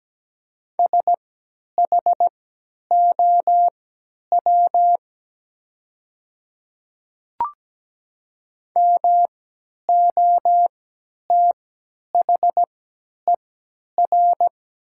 Morse code, heard as a keyed tone, about 17 wpm, spaced as for 10 wpm, 690 Hz. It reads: SHOW MOTHER
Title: 3x Character Spacing - Top 300 Words - 17wpm